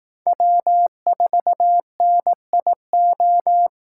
Morse code, heard as a keyed tone, about 18 wpm, 700 Hz. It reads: W4NIO
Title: Speed Racing - USA Call Signs 15wpm